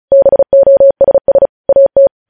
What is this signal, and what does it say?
Morse code, heard as a keyed tone, about 36 wpm, 550 Hz. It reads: BOSS AT
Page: News headlines in Morse code at 35 wpm